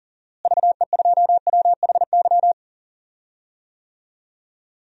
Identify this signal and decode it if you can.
Morse code, heard as a keyed tone, about 40 wpm, 700 Hz. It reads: VE2WHY
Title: DXCC-100 Call Signs - 6 Characters - Repeated 3x - 40wpm